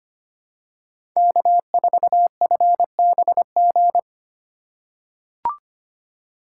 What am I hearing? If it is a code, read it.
Morse code, heard as a keyed tone, about 25 wpm, 700 Hz. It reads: K4FBG
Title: USA Call Signs - 5 Characters - 2x Character Spacing - 25wpm